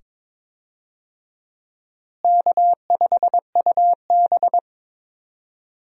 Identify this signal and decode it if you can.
Morse code, heard as a keyed tone, about 22 wpm, 700 Hz. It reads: K5UB
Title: USA Call Signs - 4 Characters - Repeated 2x - 22wpm